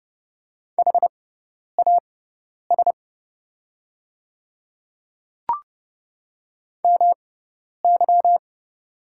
Morse code, heard as a keyed tone, about 30 wpm, spaced as for 10 wpm, 700 Hz. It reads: HAS MY
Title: 6x Character Spacing - Top 300 Words - 30wpm